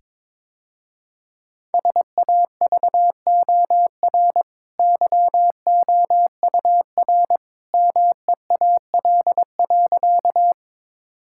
Morse code, heard as a keyed tone, about 21 wpm, 690 Hz. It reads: SAVOR YOUR MEAL.
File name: practice / 22wpm / Easy Sentences with Hints - 22wpm